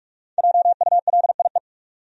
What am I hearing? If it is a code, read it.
Morse code, heard as a keyed tone, about 46 wpm, 700 Hz. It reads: JULIE
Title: USA Common Names - 45wpm